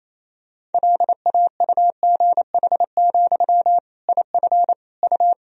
Morse code, heard as a keyed tone, about 28 wpm, 700 Hz. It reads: LAUGH, IF U
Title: Sentences from Top 500 Words and Abbreviations 28wpm